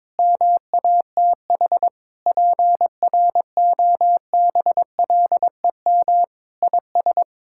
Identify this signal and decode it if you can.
Morse code, heard as a keyed tone, about 22 wpm, 700 Hz. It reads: MATH PROBLEM IS
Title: Sentences from Top 1000 Words - Review Part 6 - 22wpm